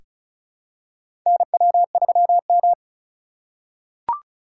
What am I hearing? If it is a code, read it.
Morse code, heard as a keyed tone, about 34 wpm, 700 Hz. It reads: NW3M
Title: USA Call Signs - 4 Characters - 2x Character Spacing - 35wpm